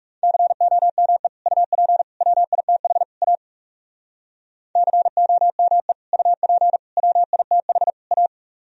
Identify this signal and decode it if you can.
Morse code, heard as a keyed tone, about 38 wpm, 700 Hz. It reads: COME UP WITH A COME UP WITH A